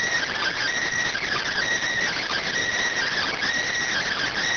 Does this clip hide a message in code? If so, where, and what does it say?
Yes: Emergency Alert System (SAME) tones from 0.3 to 4.3 s, heard as harsh buzzing bursts.